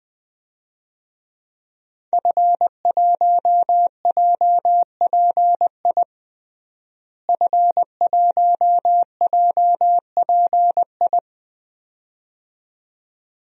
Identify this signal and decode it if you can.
Morse code, heard as a keyed tone, about 21 wpm, 700 Hz. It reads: F1JPI F1JPI
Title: DXCC-100 Call Signs - 5 Characters - Repeated 2x - 20wpm